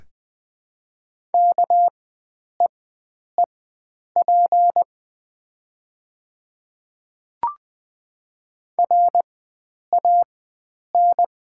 Morse code, heard as a keyed tone, about 20 wpm, spaced as for 9 wpm, 690 Hz. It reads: KEEP RAN
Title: 4x Character Spacing - Top 500 Words - 20wpm